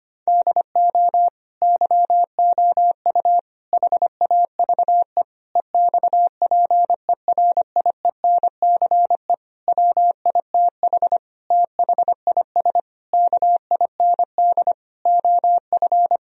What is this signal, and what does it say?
Morse code, heard as a keyed tone, about 25 wpm, 700 Hz. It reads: DO YOU HAVE EXPERIENCE WITH THIS KIND OF
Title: Sentences from Top 1000 Words - Review Part 3 - 25wpm